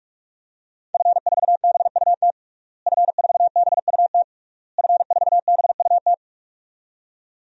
Morse code, heard as a keyed tone, about 46 wpm, 700 Hz. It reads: F4BUT F4BUT F4BUT
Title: DXCC-100 Call Signs - 5 Characters - Repeated 3x - 45wpm